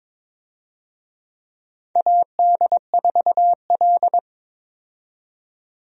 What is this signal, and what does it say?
Morse code, heard as a keyed tone, about 21 wpm, 700 Hz. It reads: AD4L
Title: USA Call Signs - 4 Characters - 22wpm